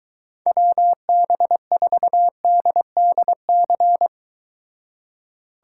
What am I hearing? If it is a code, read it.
Morse code, heard as a keyed tone, about 23 wpm, 700 Hz. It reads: WB4DDC